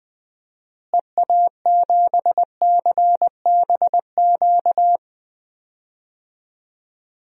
Morse code, heard as a keyed tone, about 21 wpm, 700 Hz. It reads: EA7CBQ
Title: DXCC-100 Call Signs - 6 Characters - Repeated 3x - 20wpm